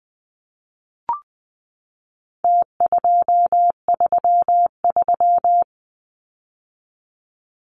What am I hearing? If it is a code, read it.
Morse code, heard as a keyed tone, about 20 wpm, 700 Hz. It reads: T233
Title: Numbers - Contest Style - 4-Digit 20wpm